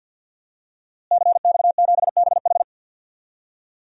Morse code, heard as a keyed tone, about 53 wpm, 700 Hz. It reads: CX7BH
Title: DXCC-100 Call Signs - 5 Characters - Repeated 2x - 50wpm